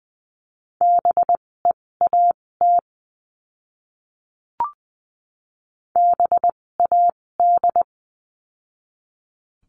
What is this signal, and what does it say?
Morse code, heard as a keyed tone, about 20 wpm, spaced as for 16 wpm, 700 Hz. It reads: BEAT BAD
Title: 256 Lesson 42a - Words 12-20wpm